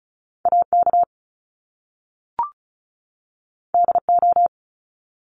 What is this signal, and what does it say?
Morse code, heard as a keyed tone, about 36 wpm, 700 Hz. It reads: AK DO
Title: Top Two Letter Combinations in Words 35wpm